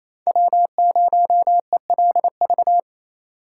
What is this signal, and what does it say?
Morse code, heard as a keyed tone, about 27 wpm, 700 Hz. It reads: W0ELV
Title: Code Groups - 5 Letter:Numbers - 28wpm